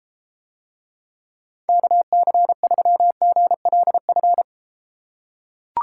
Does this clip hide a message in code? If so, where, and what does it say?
Yes: Morse code from 1.7 to 4.4 s, heard as a keyed tone, about 31 wpm, 700 Hz. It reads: KC3GLF